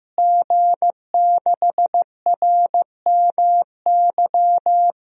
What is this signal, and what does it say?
Morse code, heard as a keyed tone, about 15 wpm, 700 Hz. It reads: G6RMY
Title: DXCC-100 Call Signs - 5 Characters - Repeated 2x - 15wpm